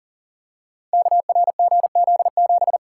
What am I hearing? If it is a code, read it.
Morse code, heard as a keyed tone, about 42 wpm, 690 Hz. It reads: KRGZ7
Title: Code Groups - 5 Letter:Numbers - 40wpm